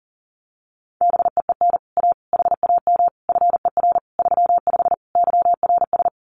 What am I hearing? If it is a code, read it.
Morse code, heard as a keyed tone, about 40 wpm, 700 Hz. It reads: BEEN A HAM FER 35 YRS
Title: Common QSO Phrases Encore 40wpm